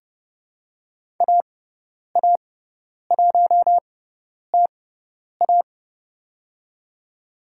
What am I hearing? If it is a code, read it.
Morse code, heard as a keyed tone, about 30 wpm, spaced as for 10 wpm, 690 Hz. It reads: AA1TA
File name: Lesson 18g - Letter m - Call Signs 30wpm